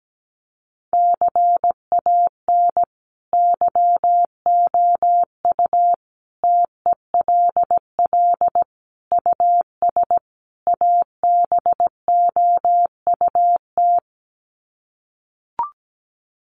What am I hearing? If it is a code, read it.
Morse code, heard as a keyed tone, about 17 wpm, 700 Hz. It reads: CAN YOU TELL US ABOUT
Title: Sets of 5 Words Encore2 17wpm